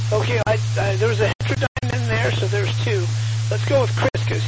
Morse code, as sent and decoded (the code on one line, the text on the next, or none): none